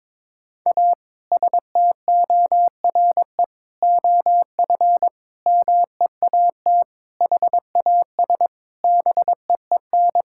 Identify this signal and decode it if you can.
Morse code, heard as a keyed tone, about 22 wpm, 700 Hz. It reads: A STORE OF MEAT HAS BEEN
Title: Sentences from Top 1000 Words - Review Part 5 - 22wpm